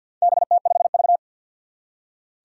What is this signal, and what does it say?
Morse code, heard as a keyed tone, about 50 wpm, 700 Hz. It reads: BT5V
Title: Four Challenging Letter-Number Combinations 50wpm